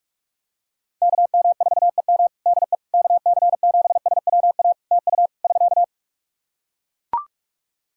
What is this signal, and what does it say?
Morse code, heard as a keyed tone, about 41 wpm, 700 Hz. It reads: KM4EM DE KC7SWA TU <SK>
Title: Practice QSO - Exchange 4 - Ending - 45wpm